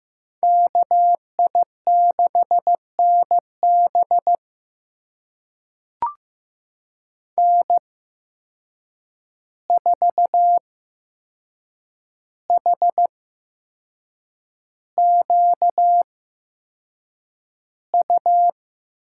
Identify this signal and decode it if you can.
Morse code, heard as a keyed tone, about 15 wpm, spaced as for 6 wpm, 700 Hz. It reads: KI6NB N4HQU
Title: USA Call Signs - 5 Characters - 8x Character Spacing - 15wpm